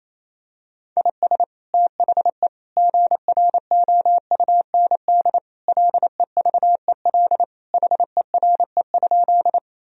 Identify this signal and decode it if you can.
Morse code, heard as a keyed tone, about 28 wpm, 700 Hz. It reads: IS THE GROUND LEVEL HERE?
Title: Sentences from Top 1000 Words - Review Part 4 - 28wpm